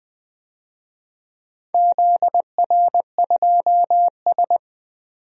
Morse code, heard as a keyed tone, about 20 wpm, 700 Hz. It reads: ZR2S
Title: DXCC-100 Call Signs - 4 Characters - 20wpm